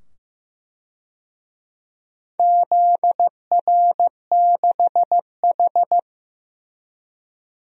Morse code, heard as a keyed tone, about 15 wpm, 700 Hz. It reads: ZR6H
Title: DXCC-100 Call Signs - 4 Characters - 15wpm